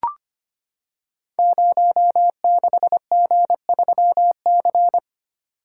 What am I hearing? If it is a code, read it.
Morse code, heard as a keyed tone, about 25 wpm, 700 Hz. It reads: 06G3C
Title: Code Groups - 5 Letter:Numbers - 25wpm